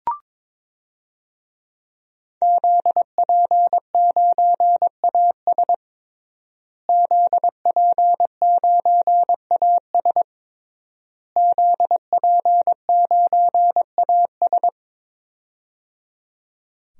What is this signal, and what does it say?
Morse code, heard as a keyed tone, about 22 wpm, 700 Hz. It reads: ZP9AS ZP9AS ZP9AS
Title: DXCC-100 Call Signs - 5 Characters - Repeated 3x - 22wpm